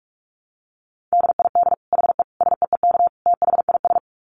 Morse code, heard as a keyed tone, about 44 wpm, 700 Hz. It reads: DID HE SEEK THIS
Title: Dit Challenge 45wpm